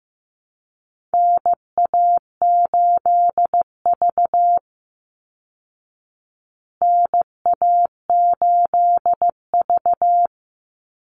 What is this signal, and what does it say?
Morse code, heard as a keyed tone, about 15 wpm, 700 Hz. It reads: NA8V NA8V
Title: Call Signs Repeated 3x - 15wpm